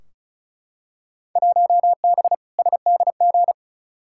Morse code, heard as a keyed tone, about 36 wpm, 700 Hz. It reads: 1B SDG
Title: Field Day Exchanges - 35wpm